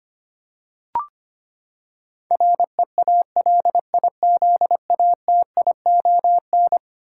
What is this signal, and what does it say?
Morse code, heard as a keyed tone, about 25 wpm, 700 Hz. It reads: REALIZATION
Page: morse-code-ninja-practice-25wpm